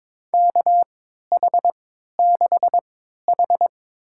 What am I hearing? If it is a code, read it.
Morse code, heard as a keyed tone, about 21 wpm, spaced as for 12 wpm, 700 Hz. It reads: KH6H
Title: USA Call Signs - 4 Characters - 3x Character Spacing - 22wpm